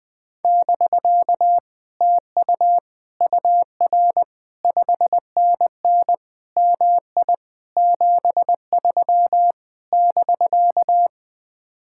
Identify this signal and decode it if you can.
Morse code, heard as a keyed tone, about 20 wpm, 700 Hz. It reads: <BK> TU UR 5NN MI 73 <BK>
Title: Speed-Racing Common QSO Phrases Encore - 17wpm